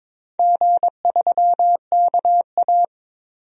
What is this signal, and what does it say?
Morse code, heard as a keyed tone, about 22 wpm, 700 Hz. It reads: G3KA
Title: DXCC-100 Call Signs - 4 Characters - 22wpm